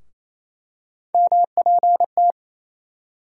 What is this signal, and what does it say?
Morse code, heard as a keyed tone, about 28 wpm, 700 Hz. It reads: MPT
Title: USA Call Sign Postfixes - 3 Letters 28wpm